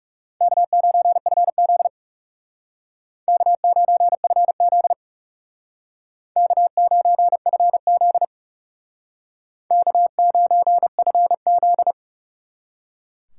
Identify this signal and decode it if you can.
Morse code, heard as a keyed tone, about 39 wpm, 700 Hz. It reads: K9FZ K9FZ K9FZ K9FZ